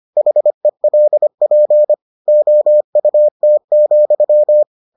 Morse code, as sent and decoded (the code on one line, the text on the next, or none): .... .-.. .--. / --- ..- - --..--
HELP OUT,